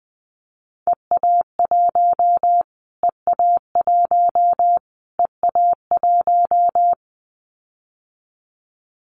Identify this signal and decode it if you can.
Morse code, heard as a keyed tone, about 20 wpm, 700 Hz. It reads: EA1 EA1 EA1
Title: Callsign Prefixes for Top 25 DXCC - Repeated 3x - 20wpm